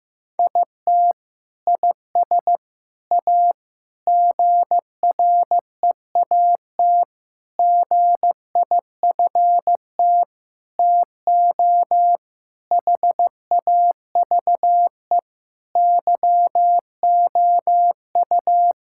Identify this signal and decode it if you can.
Morse code, heard as a keyed tone, about 15 wpm, 700 Hz. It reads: IT IS A GREAT GIFT TO HAVE YOU
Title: Sentences from Top 1000 Words - Review Part 6 - 15wpm